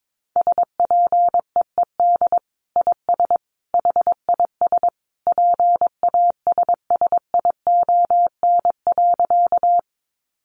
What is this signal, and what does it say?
Morse code, heard as a keyed tone, about 22 wpm, 700 Hz. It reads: SPEED IS HIS PASSION.
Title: Sentences from Top 1300 Words 22wpm